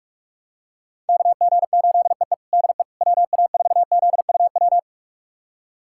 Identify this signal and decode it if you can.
Morse code, heard as a keyed tone, about 47 wpm, 700 Hz. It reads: KG8EE DE WA4ZUW
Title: USA Callsign DE Callsign 45wpm